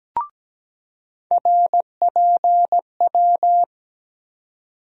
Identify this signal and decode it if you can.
Morse code, heard as a keyed tone, about 17 wpm, 700 Hz. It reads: RPW
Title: USA Call Sign Postfixes - 3 Letters 17wpm